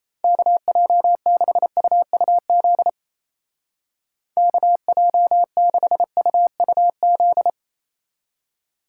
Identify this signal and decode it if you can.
Morse code, heard as a keyed tone, about 37 wpm, 700 Hz. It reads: KJ6UUZ KJ6UUZ